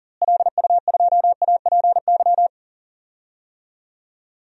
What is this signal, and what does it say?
Morse code, heard as a keyed tone, about 37 wpm, 700 Hz. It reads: LU2APY
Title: DXCC-100 Call Signs - 6 Characters - Repeated 2x - 40wpm